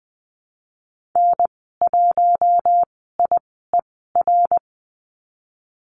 Morse code, heard as a keyed tone, about 20 wpm, spaced as for 15 wpm, 700 Hz. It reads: N1IER
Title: 055 Lesson 10g - Letter r - Call Signs 20wpm